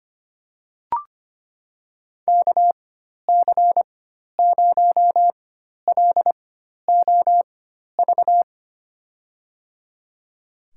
Morse code, heard as a keyed tone, about 25 wpm, spaced as for 12 wpm, 700 Hz. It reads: KC0LOV
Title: USA Call Signs - 6 Characters - 4x Character Spacing - 25wpm